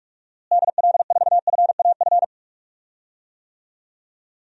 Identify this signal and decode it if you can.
Morse code, heard as a keyed tone, about 46 wpm, 700 Hz. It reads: DL4FAF
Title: DXCC-100 Call Signs - 6 Characters - Repeated 3x - 45wpm